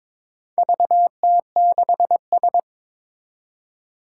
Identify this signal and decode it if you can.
Morse code, heard as a keyed tone, about 22 wpm, 700 Hz. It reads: VT6S
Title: Four Challenging Letter-Number Combinations 22wpm